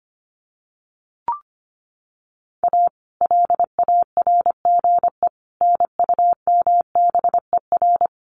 Morse code, heard as a keyed tone, about 25 wpm, 700 Hz. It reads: A LARGE NUMBER
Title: Sets of 4 Words 25wpm